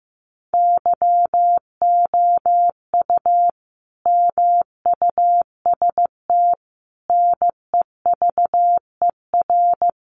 Morse code, heard as a keyed tone, about 15 wpm, 700 Hz. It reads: YOU MUST NEVER
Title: Sentences from Top 900 Words 15wpm